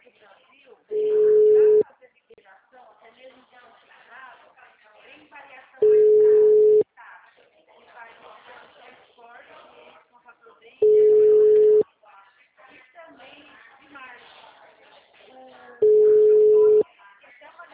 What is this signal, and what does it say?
Ringback tone (Europe, 425 Hz), with speech.